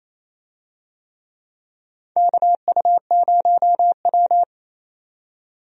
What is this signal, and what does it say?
Morse code, heard as a keyed tone, about 28 wpm, 700 Hz. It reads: KU0W
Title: USA Call Signs - 4 Characters - 28wpm